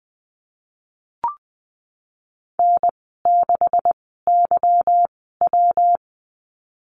Morse code, heard as a keyed tone, about 20 wpm, spaced as for 15 wpm, 700 Hz. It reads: N6YW
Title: 167 Lesson 27g - Number 7 - Call Signs 20wpm